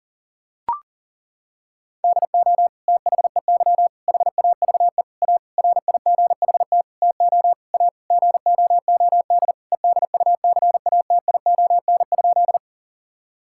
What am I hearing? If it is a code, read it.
Morse code, heard as a keyed tone, about 40 wpm, 700 Hz. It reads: DO THEY HAVE A RIGHT TO A GOOD EDUCATION?
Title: Sentences from Top 1000 Words - Review Part 6 - 40wpm